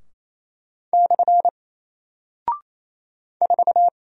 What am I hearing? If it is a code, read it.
Morse code, heard as a keyed tone, about 28 wpm, 700 Hz. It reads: / 4
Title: Single Character (Letter:Number:Punctuation:Prosign) Alt 28wpm